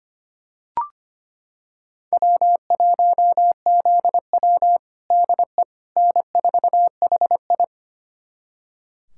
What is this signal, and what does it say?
Morse code, heard as a keyed tone, about 25 wpm, 700 Hz. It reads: W1ZW DE N4HI
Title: USA Callsign DE Callsign - 4 Characters - 25wpm